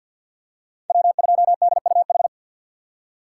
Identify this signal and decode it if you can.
Morse code, heard as a keyed tone, about 52 wpm, 700 Hz. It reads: W2DUH
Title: USA Call Signs - 5 Characters - 2x Character Spacing - 50wpm